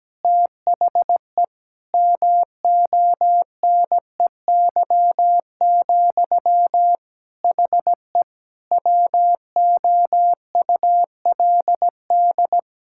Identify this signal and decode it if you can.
Morse code, heard as a keyed tone, about 17 wpm, 700 Hz. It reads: THE MONEY, HE WOULD
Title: Sentences from Top 1000 Words - Review Part 6 - 17wpm